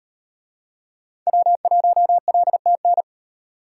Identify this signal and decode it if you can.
Morse code, heard as a keyed tone, about 38 wpm, 700 Hz. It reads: W1LTN